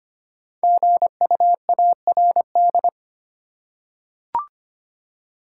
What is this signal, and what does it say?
Morse code, heard as a keyed tone, about 25 wpm, 700 Hz. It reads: GUARD
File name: 5 Letter Words - First 3 Letters Given - 25wpm